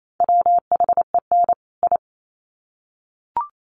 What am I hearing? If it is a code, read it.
Morse code, heard as a keyed tone, about 28 wpm, 700 Hz. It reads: WHEN I
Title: Sets of 2 Words 28wpm